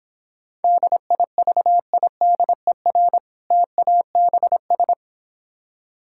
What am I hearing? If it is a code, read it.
Morse code, heard as a keyed tone, about 26 wpm, 700 Hz. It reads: DIVIDER TABS